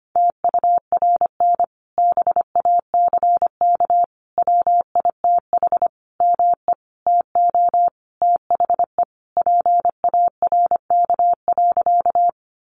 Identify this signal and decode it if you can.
Morse code, heard as a keyed tone, about 25 wpm, 700 Hz. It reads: TURN BACK WITH ME TO THE PARK.
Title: Sentences from Top 1300 Words 25wpm